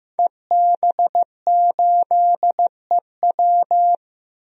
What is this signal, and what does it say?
Morse code, heard as a keyed tone, about 15 wpm, 700 Hz. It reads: EB8EW
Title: DXCC-100 Call Signs - 5 Characters - 15wpm